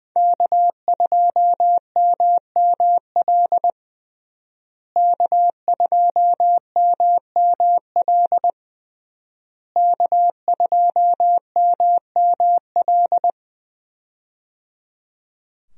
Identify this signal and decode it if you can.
Morse code, heard as a keyed tone, about 21 wpm, 700 Hz. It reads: K2MML K2MML K2MML
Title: USA Call Signs - 5 Characters - Repeated 3x - 20wpm